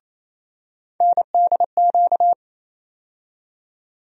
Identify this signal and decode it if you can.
Morse code, heard as a keyed tone, about 28 wpm, 700 Hz. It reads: NDQ
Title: USA Call Sign Postfixes - 3 Letters 28wpm